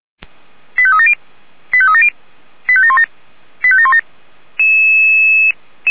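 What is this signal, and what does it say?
Five-tone selective call (ZVEI), heard as quick runs of five tones, tones of about 70 ms. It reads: 85370 85370 85717 85717
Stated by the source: ZVEI five-tone selective call received with an SDR, hosted by Feuerwehr Orlamünde.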